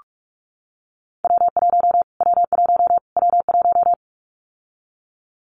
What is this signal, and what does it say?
Morse code, heard as a keyed tone, about 46 wpm, 700 Hz. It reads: W1 W1 W1
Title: Callsign Prefixes for Top 25 DXCC - Repeated 3x - 45wpm